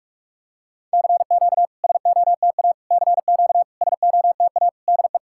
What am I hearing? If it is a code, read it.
Morse code, heard as a keyed tone, about 44 wpm, 690 Hz. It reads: CQ SOTA CQ SOTA DE
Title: SOTA QSO Practice 45wpm